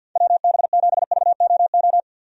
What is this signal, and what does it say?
Morse code, heard as a keyed tone, about 51 wpm, 700 Hz. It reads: WB7VOO
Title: USA Call Signs - 6 Characters - 50wpm